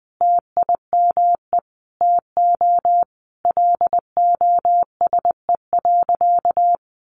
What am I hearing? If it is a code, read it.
Morse code, heard as a keyed tone, about 20 wpm, 700 Hz. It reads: TIME TO LOSE.
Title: Sentences from Top 1100 Words 20wpm